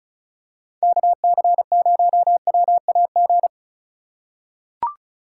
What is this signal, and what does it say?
Morse code, heard as a keyed tone, about 36 wpm, 700 Hz. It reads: KC0WAG
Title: USA Call Signs - 6 Characters - 4x Character Spacing - 35wpm